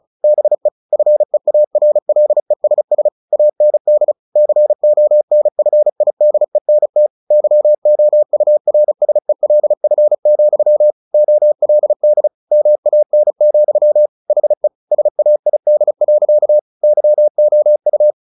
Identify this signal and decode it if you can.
Morse code, heard as a keyed tone, about 34 wpm, 600 Hz. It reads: BE FEARLESS AND CONFIDENT YOURSELF, OLD MAN, HE SAID. YOU
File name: the old man and the sea 2 35wpm